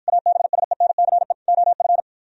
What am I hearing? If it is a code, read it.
Morse code, heard as a keyed tone, about 53 wpm, 700 Hz. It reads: ABSENCE OF